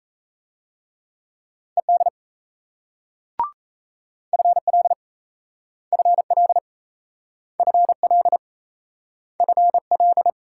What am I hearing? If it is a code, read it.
Morse code, heard as a keyed tone, about 37 wpm, 700 Hz. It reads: ED FL FL FL FL